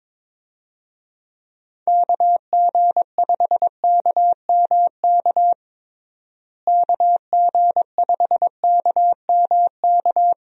Morse code, heard as a keyed tone, about 22 wpm, 700 Hz. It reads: KG5KMK KG5KMK
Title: USA Call Signs - 6 Characters - Repeated 2x - 22wpm